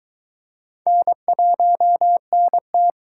Morse code, heard as a keyed tone, about 22 wpm, 700 Hz. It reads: N1NT